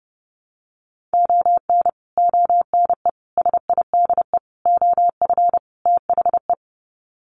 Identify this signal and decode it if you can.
Morse code, heard as a keyed tone, about 30 wpm, 700 Hz. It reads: ON ONE SIDE OF THE
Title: Sets of 5 Words 30wpm